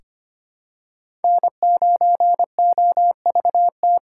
Morse code, heard as a keyed tone, about 25 wpm, 700 Hz. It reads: N9OVT
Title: USA Call Signs - 4x Character Spacing - 25wpm